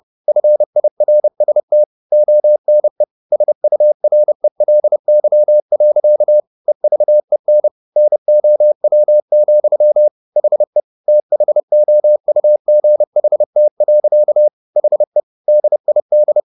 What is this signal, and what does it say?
Morse code, heard as a keyed tone, about 30 wpm, 600 Hz. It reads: FIRST ONE SURELY. EVEN NOW, HE THOUGHT. HE DID